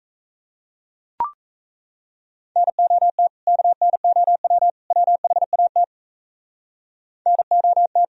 Morse code, heard as a keyed tone, about 34 wpm, 700 Hz. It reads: NOT KNOW WHAT NOT
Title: Speed-Racing - Sets of 3 Words 28wpm